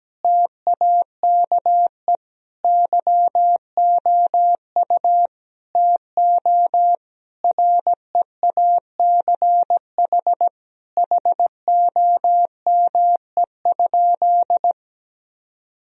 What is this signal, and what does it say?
Morse code, heard as a keyed tone, about 17 wpm, 700 Hz. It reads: TAKE YOU TO REACH HOME?
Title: Sentences from Top 1000 Words - Review Part 2 - 17wpm